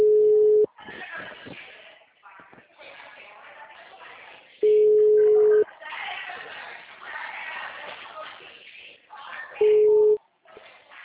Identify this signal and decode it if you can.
Ringback tone (Europe, 425 Hz), with speech.
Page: Cordeiropolis 2